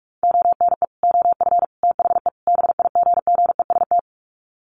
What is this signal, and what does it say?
Morse code, heard as a keyed tone, about 44 wpm, 700 Hz. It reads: ONE OF THE BIGGEST
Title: Sets of 4 Words 45wpm